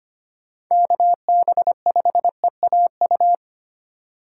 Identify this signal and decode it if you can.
Morse code, heard as a keyed tone, about 25 wpm, 700 Hz. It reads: KB5EAU
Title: USA Call Signs - 6 Characters - 2x Character Spacing - 25wpm